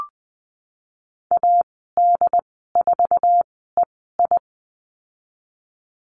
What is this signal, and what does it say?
Morse code, heard as a keyed tone, about 19 wpm, spaced as for 14 wpm, 700 Hz. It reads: AD4EI
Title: 167 Lesson 27g - Number 7 - Call Signs 20wpm